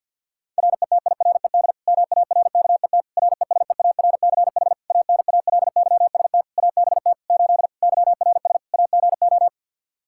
Morse code, heard as a keyed tone, about 51 wpm, 700 Hz. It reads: RETIRED MARKET RESEARCH ANALYST ABT 8 YRS AGO
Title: Practice QSO - Exchange 3 - More Info - 50wpm